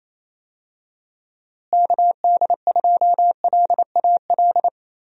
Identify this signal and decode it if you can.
Morse code, heard as a keyed tone, about 28 wpm, 700 Hz. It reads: KD2LAL
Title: USA Call Signs - 6 Characters - Repeated 2x - 28wpm